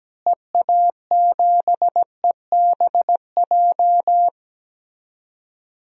Morse code, heard as a keyed tone, about 17 wpm, 700 Hz. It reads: EA7EBJ
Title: DXCC-100 Call Signs - 6 Characters - Repeated 2x - 17wpm